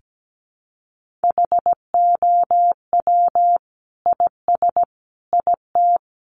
Morse code, heard as a keyed tone, about 17 wpm, 700 Hz. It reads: HOW IS IT
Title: Sentences from Top 900 Words 17wpm